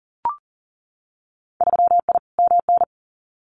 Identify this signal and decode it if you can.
Morse code, heard as a keyed tone, about 40 wpm, 700 Hz. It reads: 3I MN
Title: Winter Field Day Exchanges 40wpm